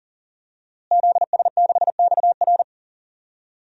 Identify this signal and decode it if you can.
Morse code, heard as a keyed tone, about 40 wpm, 700 Hz. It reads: ZS6XR
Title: DXCC-100 Call Signs - 5 Characters - Repeated 3x - 40wpm